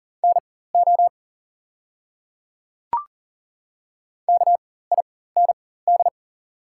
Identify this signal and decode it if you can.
Morse code, heard as a keyed tone, about 38 wpm, spaced as for 18 wpm, 700 Hz. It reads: NO KIND
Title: 4x Character Spacing - Top 200 Words - 40wpm